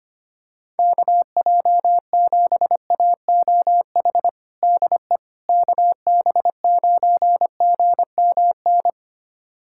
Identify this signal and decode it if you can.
Morse code, heard as a keyed tone, about 25 wpm, 700 Hz. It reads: KJ7AOH DE KB9GMN
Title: USA Callsign DE Callsign - 6 Characters - 25wpm